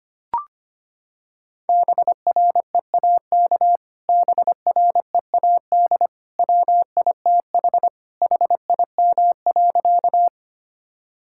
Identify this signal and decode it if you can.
Morse code, heard as a keyed tone, about 25 wpm, 700 Hz. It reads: BREAK BREAD WITH HIM.
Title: Sentences from Top 1600 Words 25wpm